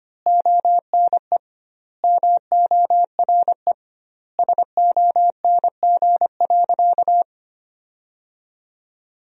Morse code, heard as a keyed tone, about 25 wpm, 700 Hz. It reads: ONE MORE SONG.
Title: 2x Word-Spacing - Sentences from Top 400 Words - 25wpm